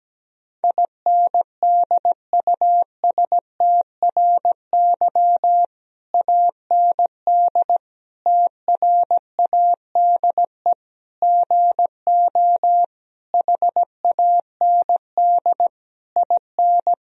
Morse code, heard as a keyed tone, about 17 wpm, 700 Hz. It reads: INDUSTRY AND TRADE GO HAND IN
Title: Sentences from Top 1000 Words - Review Part 4 - 17wpm